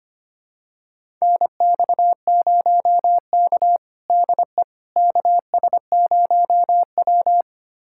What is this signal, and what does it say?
Morse code, heard as a keyed tone, about 25 wpm, 700 Hz. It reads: NX0K DE KS0W
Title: USA Callsign DE Callsign - 4 Characters - 25wpm